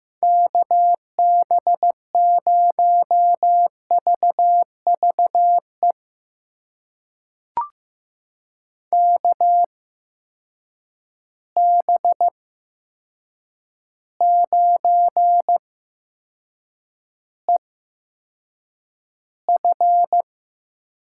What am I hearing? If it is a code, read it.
Morse code, heard as a keyed tone, about 15 wpm, spaced as for 7 wpm, 700 Hz. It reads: KB0VVE KB9EF
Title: USA Call Signs - 8x Character Spacing - 15wpm